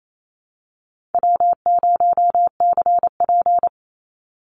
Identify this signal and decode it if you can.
Morse code, heard as a keyed tone, about 28 wpm, 700 Hz. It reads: W0CP
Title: Easy Call Signs - 28wpm